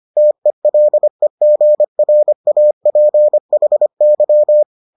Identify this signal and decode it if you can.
Morse code, heard as a keyed tone, about 25 wpm, 600 Hz. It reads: TELEGRAPHY